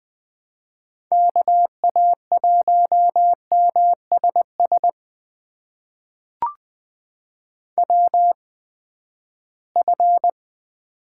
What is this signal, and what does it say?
Morse code, heard as a keyed tone, about 20 wpm, spaced as for 14 wpm, 700 Hz. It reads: KA1MSS WF
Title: USA Call Signs - 8x Character Spacing - 20wpm